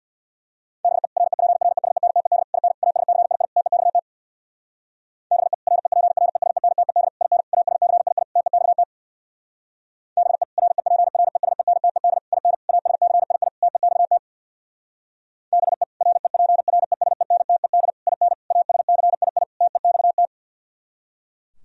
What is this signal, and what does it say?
Morse code, heard as a keyed tone, about 63 wpm, 690 Hz. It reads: BE REPRESENTED IN ASCII TEXT BE REPRESENTED IN ASCII TEXT BE REPRESENTED IN ASCII TEXT BE REPRESENTED IN ASCII TEXT